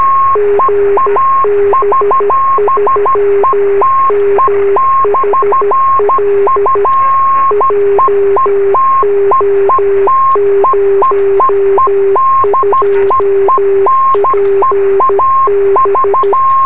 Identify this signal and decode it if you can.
Morse code, heard as a keyed tone, about 13 wpm, 400 Hz, with radio noise. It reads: GB3MHL JO02PB
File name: sm7lcb / 20 23cm / 2025 gb3mhl beacon 23cm